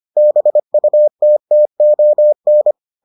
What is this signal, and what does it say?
Morse code, heard as a keyed tone, about 25 wpm, 600 Hz. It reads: BUTTON